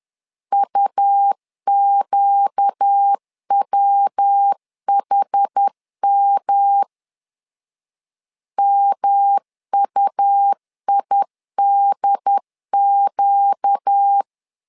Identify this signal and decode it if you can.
Morse code, heard as a keyed tone, about 10 wpm, 800 Hz. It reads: UQWHM MUIDQ
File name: LEZIONI / 4 Q 10wpm 2210